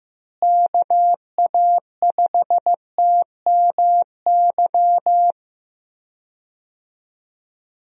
Morse code, heard as a keyed tone, about 16 wpm, 700 Hz. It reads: KA5TMY
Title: USA Call Signs - 6 Characters - Repeated 3x - 15wpm